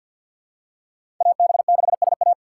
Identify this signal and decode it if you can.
Morse code, heard as a keyed tone, about 52 wpm, 700 Hz. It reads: AB6SA
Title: USA Call Signs - 5 Characters - 2x Character Spacing - 50wpm